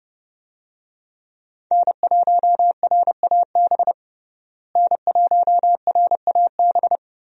Morse code, heard as a keyed tone, about 30 wpm, 700 Hz. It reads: N1RAB N1RAB
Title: USA Call Signs - 5 Characters - Repeated 2x - 30wpm